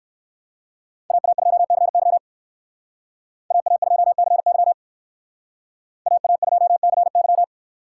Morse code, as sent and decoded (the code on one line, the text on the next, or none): .- .- ..--- -.-. -.-- / .- .- ..--- -.-. -.-- / .- .- ..--- -.-. -.--
AA2CY AA2CY AA2CY